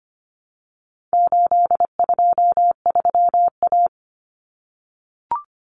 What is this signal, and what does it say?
Morse code, heard as a keyed tone, about 25 wpm, 700 Hz. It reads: 823A
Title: Common Cut Numbers - 4-Digit 25wpm